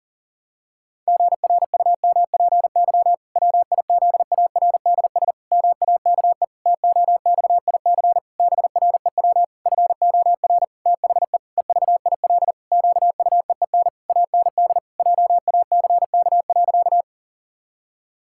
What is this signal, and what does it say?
Morse code, heard as a keyed tone, about 40 wpm, 700 Hz. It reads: GRUMPY WIZARDS MAKE TOXIC BREW FOR THE EVIL QUEEN AND JACK.
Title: Pangram Challenge v2 - 40wpm